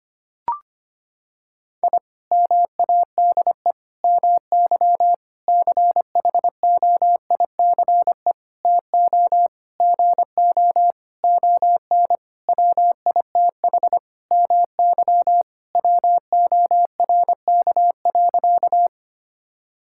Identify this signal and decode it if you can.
Morse code, heard as a keyed tone, about 25 wpm, 700 Hz. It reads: I MADE MY CHOICE TO GO ON WITH MY WORK.
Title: Sentences from Top 1000 Words - Review Part 6 - 25wpm